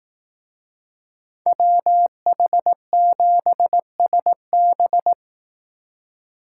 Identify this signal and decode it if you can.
Morse code, heard as a keyed tone, about 18 wpm, 700 Hz. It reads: WH7SB
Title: Speed Racing - USA 2x2 Call Signs 15wpm